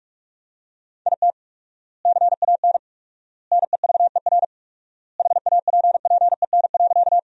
Morse code, heard as a keyed tone, about 46 wpm, 700 Hz. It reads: IT CAN NEVER HAPPEN.
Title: 4x Word-Spacing - Sentences from Top 400 Words - 45wpm